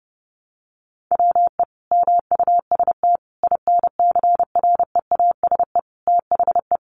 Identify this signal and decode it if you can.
Morse code, heard as a keyed tone, about 30 wpm, 700 Hz. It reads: WE MUST INCREASE THE
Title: Sentences from Top 1300 Words 30wpm